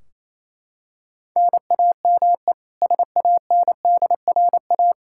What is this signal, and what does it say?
Morse code, heard as a keyed tone, about 27 wpm, 700 Hz. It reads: NAME SANDRA
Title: Practice QSO - Exchange 1 - Beginning - 28wpm